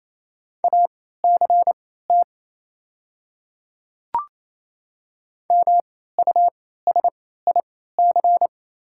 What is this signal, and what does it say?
Morse code, heard as a keyed tone, about 28 wpm, spaced as for 16 wpm, 700 Hz. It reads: ACT MUSIC
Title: 3x Character Spacing - Top 300 Words - 28wpm